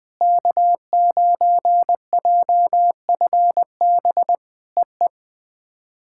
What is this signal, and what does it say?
Morse code, heard as a keyed tone, about 21 wpm, 700 Hz. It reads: K9JFB EE
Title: POTA QSO Practice - 20wpm